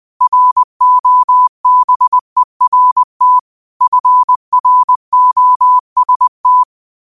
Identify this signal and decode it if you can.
Morse code, heard as a keyed tone, about 20 wpm, 1000 Hz. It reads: ROBERT FROST
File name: qotdfiles / 20wpm